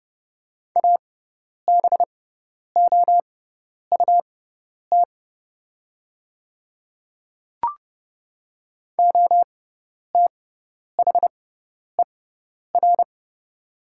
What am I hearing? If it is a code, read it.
Morse code, heard as a keyed tone, about 29 wpm, spaced as for 10 wpm, 690 Hz. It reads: ABOUT OTHER
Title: 6x Character Spacing - Top 300 Words - 30wpm